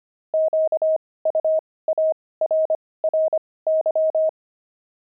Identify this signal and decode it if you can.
Morse code, heard as a keyed tone, about 25 wpm, spaced as for 18 wpm, 630 Hz. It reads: QUARRY
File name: pangram-fs-18-cs-25-steaks